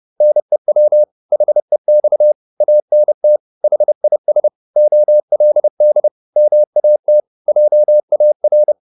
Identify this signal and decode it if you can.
Morse code, heard as a keyed tone, about 30 wpm, 600 Hz. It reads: NEW HEX ANT HIS OLD MAT JAR